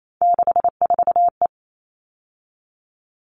Morse code, heard as a keyed tone, about 28 wpm, 700 Hz. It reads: 64E
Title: All Cut Numbers - 3-Digit 28wpm